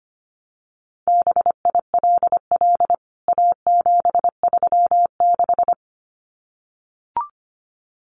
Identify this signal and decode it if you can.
Morse code, heard as a keyed tone, about 25 wpm, 700 Hz. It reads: BILL A736
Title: CWT Exchange 25wpm